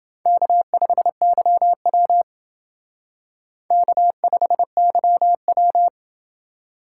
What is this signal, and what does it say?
Morse code, heard as a keyed tone, about 29 wpm, 700 Hz. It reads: K5YW K5YW